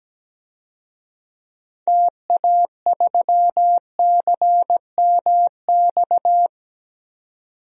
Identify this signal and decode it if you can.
Morse code, heard as a keyed tone, about 17 wpm, 700 Hz. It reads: TA3CMX
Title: DXCC-100 Call Signs - 6 Characters - 17wpm